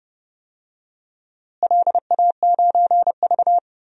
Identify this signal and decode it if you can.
Morse code, heard as a keyed tone, about 32 wpm, 700 Hz. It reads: LA9V
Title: DXCC-100 Call Signs - 4 Characters - Repeated 3x - 30wpm